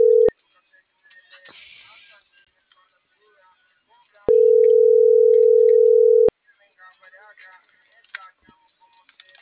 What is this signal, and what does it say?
Ringback tone (North America), 440 + 480 Hz.